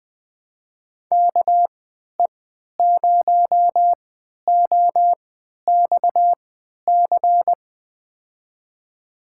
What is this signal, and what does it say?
Morse code, heard as a keyed tone, about 20 wpm, spaced as for 11 wpm, 700 Hz. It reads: KE0OXC
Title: USA Call Signs - 3x Character Spacing - 20wpm